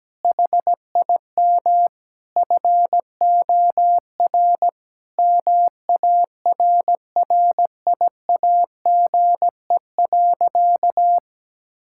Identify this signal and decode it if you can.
Morse code, heard as a keyed tone, about 17 wpm, 700 Hz. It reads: HIM FOR MARRIAGE.
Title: Sentences from Top 1000 Words - Review Part 6 - 17wpm